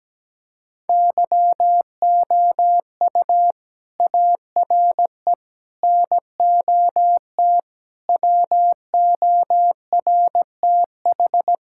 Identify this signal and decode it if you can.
Morse code, heard as a keyed tone, about 17 wpm, 700 Hz. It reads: YOU ARE NOT WORTH
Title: Sentences from Top 1600 Words 17wpm